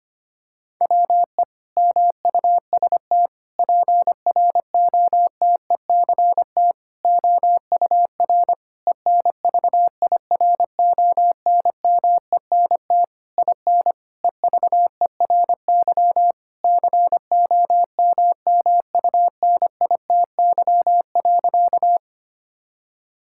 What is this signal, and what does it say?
Morse code, heard as a keyed tone, about 25 wpm, 700 Hz. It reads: WE MUST PROTECT OUR ENVIRONMENT IN EVERY COMMUNITY.
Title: Sentences from Top 1000 Words - Review Part 5 - 25wpm